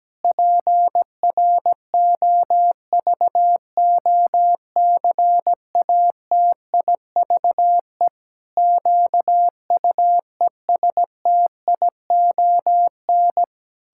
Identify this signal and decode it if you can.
Morse code, heard as a keyed tone, about 17 wpm, 700 Hz. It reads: PROVOCATIVE QUESTION
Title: Pangram Challenge v2 - 17wpm